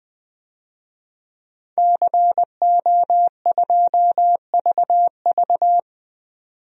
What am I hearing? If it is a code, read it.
Morse code, heard as a keyed tone, about 20 wpm, 700 Hz. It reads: CO2VV
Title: DXCC-100 Call Signs - 5 Characters - 20wpm